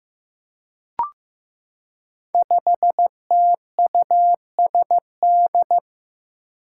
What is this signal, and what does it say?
Morse code, heard as a keyed tone, about 15 wpm, 700 Hz. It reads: 5TUSD
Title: Five Challenging Letter-Number Combinations 15wpm